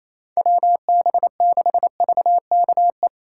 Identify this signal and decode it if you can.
Morse code, heard as a keyed tone, about 35 wpm, 700 Hz. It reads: WB6VKE